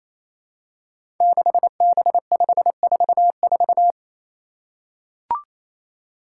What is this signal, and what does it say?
Morse code, heard as a keyed tone, about 27 wpm, 700 Hz. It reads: 6B544
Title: Five Challenging Letter-Number Combinations 28wpm